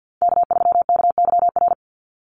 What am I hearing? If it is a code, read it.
Morse code, heard as a keyed tone, about 48 wpm, 700 Hz. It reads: K3KYR
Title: NA SOTA Chaser Callsigns 50wpm